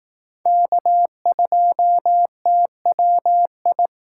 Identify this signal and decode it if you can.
Morse code, heard as a keyed tone, about 19 wpm, 700 Hz. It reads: K2TWI
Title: Speed Racing - USA 1x3 Call Signs 15wpm